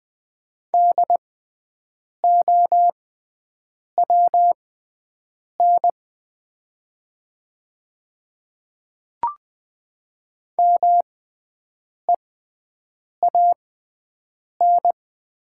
Morse code, heard as a keyed tone, about 20 wpm, spaced as for 7 wpm, 700 Hz. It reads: DOWN MEAN
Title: 6x Character Spacing - Top 200 Words - 20wpm